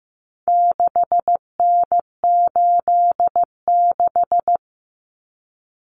Morse code, heard as a keyed tone, about 15 wpm, 700 Hz. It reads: 6N86
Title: Common Cut Numbers - 4-Digit 15wpm